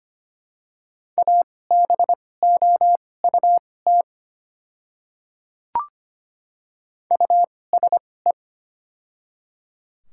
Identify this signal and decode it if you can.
Morse code, heard as a keyed tone, about 25 wpm, spaced as for 18 wpm, 700 Hz. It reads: ABOUT USE
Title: 2x Character Spacing - Top 100 Words - 25wpm